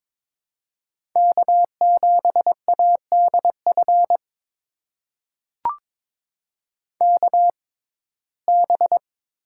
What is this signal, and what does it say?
Morse code, heard as a keyed tone, about 22 wpm, spaced as for 16 wpm, 700 Hz. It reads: K7ADF KB
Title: USA Call Signs - 5 Characters - 6x Character Spacing - 22wpm